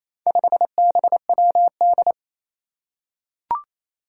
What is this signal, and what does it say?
Morse code, heard as a keyed tone, about 28 wpm, 700 Hz. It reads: 5BWD
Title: Four Challenging Letter-Number Combinations 28wpm